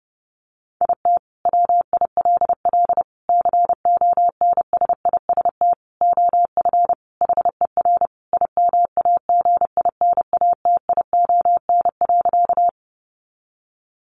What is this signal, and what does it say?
Morse code, heard as a keyed tone, about 30 wpm, 700 Hz. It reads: IT WILL CONSIST OF HER IMAGINATION.
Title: Sentences from Top 1400 Words 30wpm